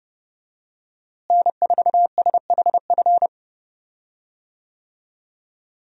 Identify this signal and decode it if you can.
Morse code, heard as a keyed tone, about 29 wpm, 700 Hz. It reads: N4SHF
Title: USA Call Signs - 5 Characters - Repeated 2x - 30wpm